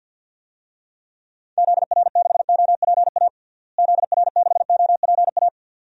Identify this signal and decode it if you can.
Morse code, heard as a keyed tone, about 52 wpm, 700 Hz. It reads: ZR6OPA ZR6OPA
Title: DXCC-100 Call Signs - 6 Characters - Repeated 2x - 50wpm